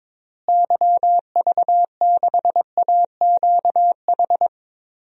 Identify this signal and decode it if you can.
Morse code, heard as a keyed tone, about 22 wpm, 700 Hz. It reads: YV6AQH
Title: DXCC-100 Call Signs - 6 Characters - Repeated 2x - 22wpm